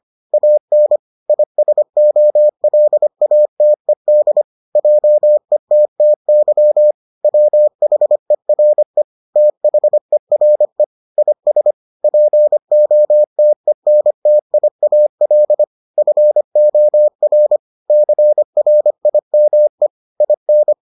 Morse code, heard as a keyed tone, about 25 wpm, 600 Hz. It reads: AN ISOLATED JETTY WHERE THERE IS POTENTIAL FOR CRIME IN